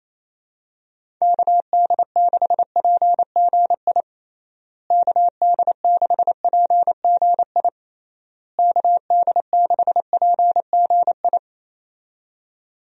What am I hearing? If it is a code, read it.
Morse code, heard as a keyed tone, about 28 wpm, 700 Hz. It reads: KD6PGI KD6PGI KD6PGI
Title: USA Call Signs - 6 Characters - Repeated 3x - 28wpm